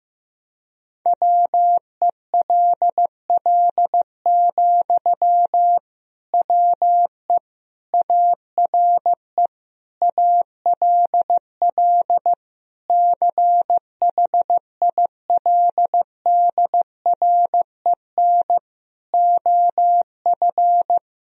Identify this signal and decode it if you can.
Morse code, heard as a keyed tone, about 15 wpm, 700 Hz. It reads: WELL, WE ARE ALL CHILDREN OF
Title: Sentences from Top 1000 Words - Review Part 4 - 15wpm